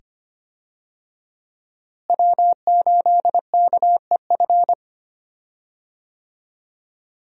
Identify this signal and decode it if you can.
Morse code, heard as a keyed tone, about 25 wpm, 700 Hz. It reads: W8KEF
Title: USA Call Signs - 5 Characters - Repeated 2x - 25wpm